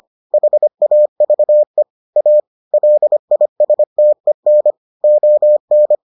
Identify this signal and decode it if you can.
Morse code, heard as a keyed tone, about 25 wpm, 600 Hz. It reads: HAVE A LISTEN ON